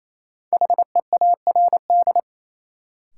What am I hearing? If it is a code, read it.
Morse code, heard as a keyed tone, about 28 wpm, 700 Hz. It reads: HEARD
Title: Sentences from Top 400 Words - New Word Drill - 28wpm